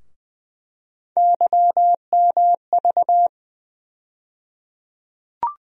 Morse code, heard as a keyed tone, about 20 wpm, 700 Hz. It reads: YMV
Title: USA Call Sign Postfixes - 3 Letters 20wpm